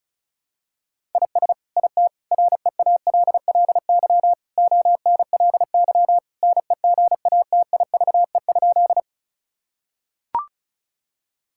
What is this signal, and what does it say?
Morse code, heard as a keyed tone, about 34 wpm, 700 Hz. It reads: IS IT REALLY ONLY NEGATIVE?
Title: Sentences from Top 1000 Words - Review Part 6 - 35wpm